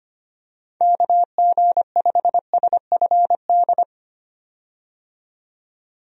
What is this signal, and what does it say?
Morse code, heard as a keyed tone, about 25 wpm, 700 Hz. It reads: KG5SFD
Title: USA Call Signs - 6 Characters - 25wpm